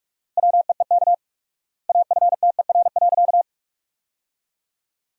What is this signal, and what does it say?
Morse code, heard as a keyed tone, about 46 wpm, 700 Hz. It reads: WEEK AFTER.
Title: 4x Word-Spacing - Sentences from Top 500 Words - 45wpm